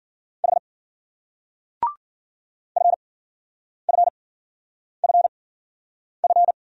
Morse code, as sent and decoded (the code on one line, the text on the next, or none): .... / ..-. ..-. ..-. ..-.
H FFFF